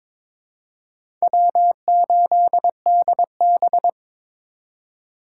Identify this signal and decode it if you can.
Morse code, heard as a keyed tone, about 22 wpm, 700 Hz. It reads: W8DB
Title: USA Call Signs - 4x Character Spacing - 22wpm